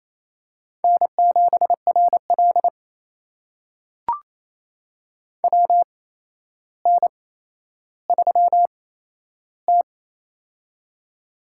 Morse code, heard as a keyed tone, about 28 wpm, spaced as for 12 wpm, 700 Hz. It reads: N7RL WN3T
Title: USA Call Signs - 4 Characters - 8x Character Spacing - 28wpm